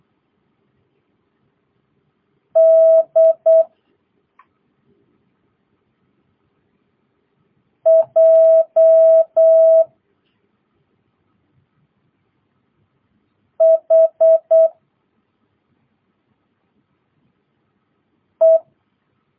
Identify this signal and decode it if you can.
Morse code, heard as a keyed tone, about 8 wpm, 650 Hz. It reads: D J H E
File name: jh8cbh / 2022cw